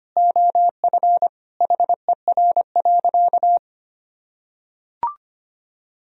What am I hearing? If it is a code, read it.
Morse code, heard as a keyed tone, about 25 wpm, 700 Hz. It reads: OF HER.
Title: Sentences from Top 1000 Words - Review Part 2 - 25wpm